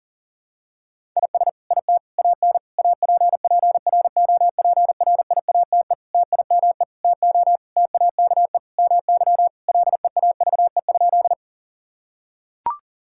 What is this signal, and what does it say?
Morse code, heard as a keyed tone, about 40 wpm, 700 Hz. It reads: IS IT AN APPROPRIATE TIME TO TAKE MY LEAVE?
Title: Sentences from Top 1000 Words - Review Part 5 - 40wpm